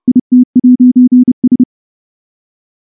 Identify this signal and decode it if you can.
Morse code, heard as a keyed tone, about 30 wpm, 250 Hz. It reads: IT'S
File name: morse-training-30WPM